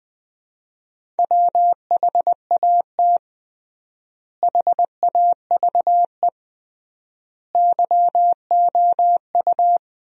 Morse code, heard as a keyed tone, about 20 wpm, 700 Hz. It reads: WHAT HAVE YOU
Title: 3x Word-Spacing - Sentences from Top 400 Words - 20wpm